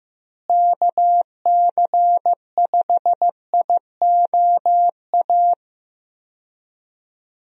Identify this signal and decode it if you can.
Morse code, heard as a keyed tone, about 15 wpm, 700 Hz. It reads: KC5IOA
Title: USA Call Signs - 6 Characters - Repeated 3x - 15wpm